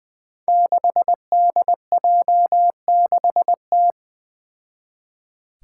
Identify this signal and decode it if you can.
Morse code, heard as a keyed tone, about 21 wpm, 700 Hz. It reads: 6DJ6T
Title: Five Challenging Letter-Number Combinations 20wpm